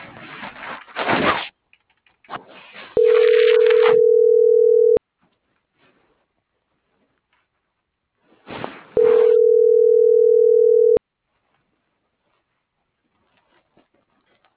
Ringback tone (North America), 440 + 480 Hz.